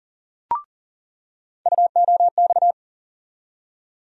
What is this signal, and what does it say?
Morse code, heard as a keyed tone, about 40 wpm, 700 Hz. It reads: UOX
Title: USA Call Sign Postfixes - 3 Letters 40wpm